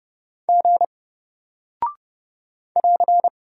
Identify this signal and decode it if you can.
Morse code, heard as a keyed tone, about 30 wpm, 700 Hz. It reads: G <AR>
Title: Single Character (Letter:Number:Punctuation:Prosign) Alt 30wpm